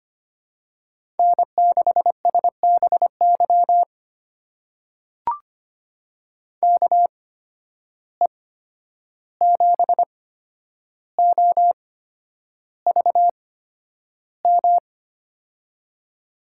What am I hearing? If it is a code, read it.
Morse code, heard as a keyed tone, about 25 wpm, spaced as for 10 wpm, 700 Hz. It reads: N6SBY KE7OVM
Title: USA Call Signs - 8x Character Spacing - 25wpm